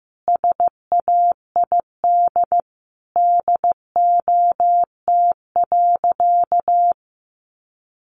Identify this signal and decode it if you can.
Morse code, heard as a keyed tone, about 15 wpm, 700 Hz. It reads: SAID DOT.
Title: Sentences from Top 1500 Words 15wpm